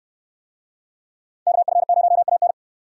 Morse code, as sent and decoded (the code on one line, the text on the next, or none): -.- ..-. ----- .- -.
KF0AN